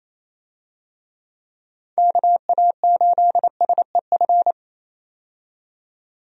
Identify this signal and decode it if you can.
Morse code, heard as a keyed tone, about 27 wpm, 700 Hz. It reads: KA8SEF
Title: USA Call Signs - 6 Characters - 28wpm